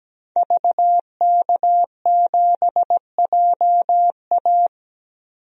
Morse code, heard as a keyed tone, about 17 wpm, 700 Hz. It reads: VK7JA
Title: DXCC-100 Call Signs - 5 Characters - 17wpm